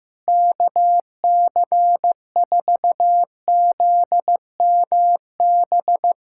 Morse code, heard as a keyed tone, about 16 wpm, 700 Hz. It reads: KC4ZMB
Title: USA Callsign DE Callsign - 6 Characters - 15wpm